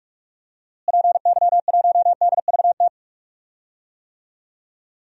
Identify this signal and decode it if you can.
Morse code, heard as a keyed tone, about 41 wpm, 700 Hz. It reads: PY1DVT
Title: DXCC-100 Call Signs - 6 Characters - Repeated 3x - 45wpm